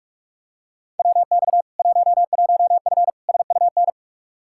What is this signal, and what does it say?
Morse code, heard as a keyed tone, about 42 wpm, 700 Hz. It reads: WX 11F SUN
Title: Practice QSO - Exchange 2 - Info - 45wpm